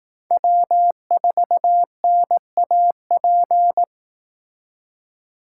Morse code, heard as a keyed tone, about 18 wpm, 700 Hz. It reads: W4NAP
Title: Speed Racing - USA 1x3 Call Signs 15wpm